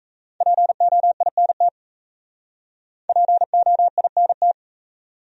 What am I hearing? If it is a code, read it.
Morse code, heard as a keyed tone, about 34 wpm, 700 Hz. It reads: POINT POINT
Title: Speed-Racing Top 500 Words 28wpm